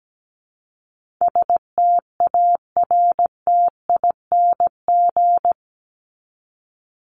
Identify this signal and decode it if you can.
Morse code, heard as a keyed tone, about 17 wpm, 700 Hz. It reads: STARTING
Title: ING Suffix - Top 100 Words with Base Words 17wpm